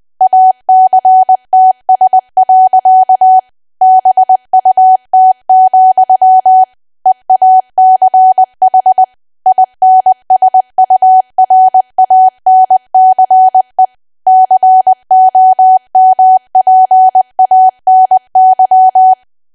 Morse code, heard as a keyed tone, about 20 wpm, 750 Hz. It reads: ACTS. BUT, EACH INSURANCE COMPANY
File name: cwa / W1AW-Level3 / 20WPM